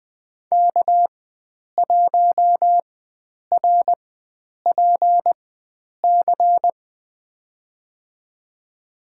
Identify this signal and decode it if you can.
Morse code, heard as a keyed tone, about 20 wpm, spaced as for 9 wpm, 700 Hz. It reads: K1RPC
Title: USA Call Signs - 5 Characters - 4x Character Spacing - 20wpm